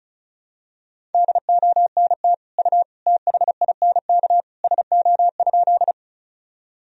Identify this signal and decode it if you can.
Morse code, heard as a keyed tone, about 36 wpm, 700 Hz. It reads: DONT U THINK SO?
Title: Sentences from Top 300 Words and Abbreviations 35wpm